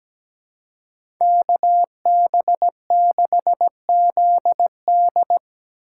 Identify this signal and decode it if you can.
Morse code, heard as a keyed tone, about 17 wpm, 700 Hz. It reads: KB6ZD
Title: USA Call Signs - 2x Character Spacing - 17wpm